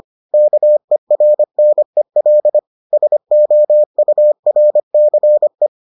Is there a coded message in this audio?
Yes: Morse code from 0.3 to 5.7 s, heard as a keyed tone, about 25 wpm, 600 Hz. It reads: KERNEL SOURCE